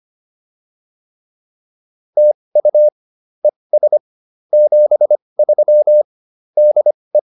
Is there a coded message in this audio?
Yes: Morse code from 2.2 to 7.2 s, heard as a keyed tone, about 25 wpm, spaced as for 20 wpm, 600 Hz. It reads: TU ES 73 DE